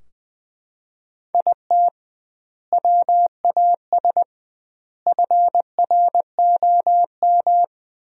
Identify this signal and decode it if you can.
Morse code, heard as a keyed tone, about 20 wpm, 700 Hz. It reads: IT WAS FROM
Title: 2x Word-Spacing - Sentences from Top 100 Words - 20wpm